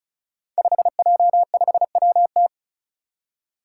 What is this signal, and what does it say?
Morse code, heard as a keyed tone, about 36 wpm, 700 Hz. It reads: 5J5WT
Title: Five Challenging Letter-Number Combinations 35wpm